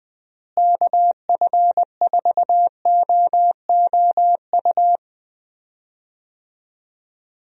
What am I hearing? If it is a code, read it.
Morse code, heard as a keyed tone, about 20 wpm, 700 Hz. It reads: KF4OOU
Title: USA Call Signs - 6 Characters - Repeated 2x - 20wpm